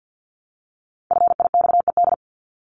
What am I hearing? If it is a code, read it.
Morse code, heard as a keyed tone, about 50 wpm, 700 Hz. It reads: FIXED
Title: ED Suffix - Top 100 Words with Base Words 50wpm